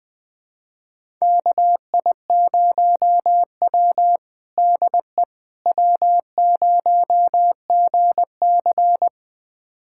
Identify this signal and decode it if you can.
Morse code, heard as a keyed tone, about 21 wpm, 700 Hz. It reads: KI0W DE W0GC
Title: USA Callsign DE Callsign - 4 Characters - 20wpm